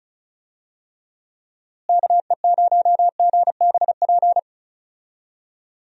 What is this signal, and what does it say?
Morse code, heard as a keyed tone, about 36 wpm, 700 Hz. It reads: KE0GBP
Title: USA Call Signs - 6 Characters - Repeated 3x - 35wpm